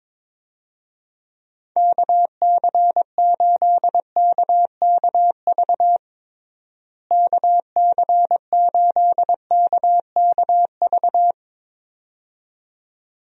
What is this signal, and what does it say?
Morse code, heard as a keyed tone, about 22 wpm, 700 Hz. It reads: KC8KKV KC8KKV
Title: USA Call Signs - 6 Characters - Repeated 2x - 22wpm